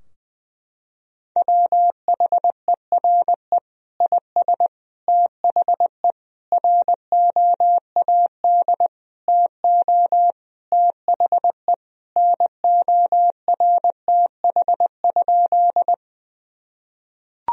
Morse code, heard as a keyed tone, about 20 wpm, 700 Hz. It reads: WHERE IS THE ROAD TO THE NORTH?
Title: Sentences from Top 1000 Words - Review Part 2 - 20wpm